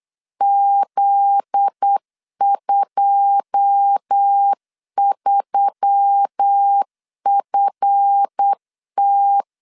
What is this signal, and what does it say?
Morse code, heard as a keyed tone, about 8 wpm, 800 Hz. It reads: Z23FT
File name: LEZIONI / 9 2 8wpm 3437